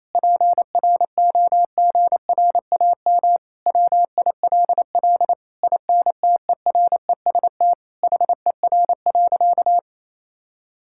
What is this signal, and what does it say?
Morse code, heard as a keyed tone, about 28 wpm, 700 Hz. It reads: PROGRAM WILL INTEREST HER.
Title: Sentences from Top 1000 Words - Review Part 5 - 28wpm